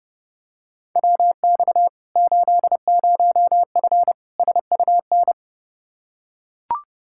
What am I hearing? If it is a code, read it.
Morse code, heard as a keyed tone, about 30 wpm, 700 Hz. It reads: WX 80F SUN
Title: Practice QSO - Exchange 2 - Info - 30wpm